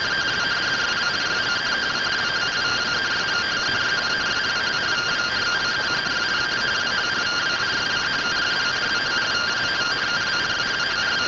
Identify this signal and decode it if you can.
Two-tone teleprinter signal, about 170 Hz shift.
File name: findings / CT AXIAL -111.gen-40 rtty rx